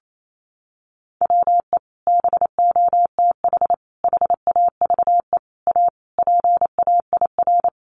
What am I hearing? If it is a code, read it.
Morse code, heard as a keyed tone, about 28 wpm, 700 Hz. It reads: WE BOTH HAVE A PAIR
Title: Sentences from Top 1200 Words 28wpm